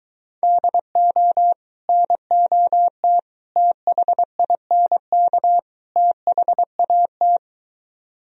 Morse code, heard as a keyed tone, about 23 wpm, 700 Hz. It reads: DO NOT THINK THAT